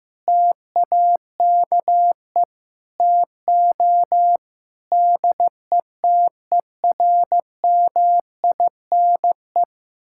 Morse code, heard as a keyed tone, about 15 wpm, 700 Hz. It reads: TAKE TO DETERMINE
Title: Sentences from Top 1000 Words - Review Part 5 - 15wpm